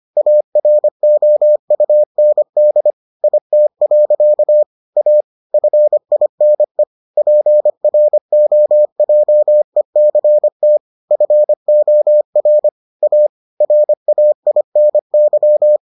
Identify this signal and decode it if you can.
Morse code, heard as a keyed tone, about 25 wpm, 600 Hz. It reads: AROUND IT. A FINE PROJECT FOR A RAINY